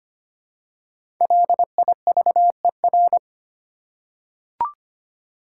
Morse code, heard as a keyed tone, about 25 wpm, 700 Hz. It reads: LIVER
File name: 5 Letter Words - First 3 Letters Given - 25wpm